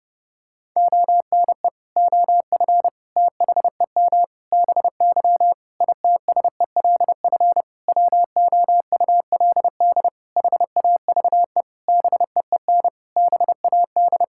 Morse code, heard as a keyed tone, about 30 wpm, 700 Hz. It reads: ONE OF THEM BY ITSELF WOULD HAVE BEEN BAD
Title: Sentences from Top 1000 Words - Review Part 5 - 30wpm